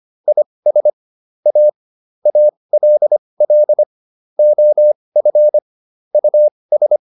Morse code, heard as a keyed tone, about 25 wpm, spaced as for 20 wpm, 600 Hz. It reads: IS A ALL OF US